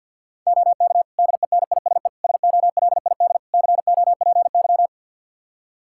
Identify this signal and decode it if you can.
Morse code, heard as a keyed tone, about 52 wpm, 700 Hz. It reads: OK DENISE SOLID COPY
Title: Practice QSO - Exchange 3 - More Info - 50wpm